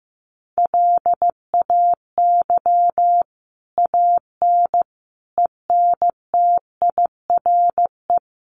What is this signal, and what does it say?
Morse code, heard as a keyed tone, about 15 wpm, 700 Hz. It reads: LAY AN ENTIRE
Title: A Sound Of Thunder 15wpm